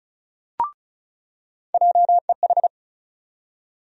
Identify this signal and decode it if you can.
Morse code, heard as a keyed tone, about 36 wpm, 700 Hz. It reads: JEH
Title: USA Call Sign Postfixes - 3 Letters 35wpm